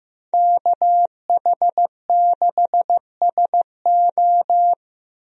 Morse code, heard as a keyed tone, about 15 wpm, 700 Hz. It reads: KH6SO
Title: DXCC-100 Call Signs - 5 Characters - 15wpm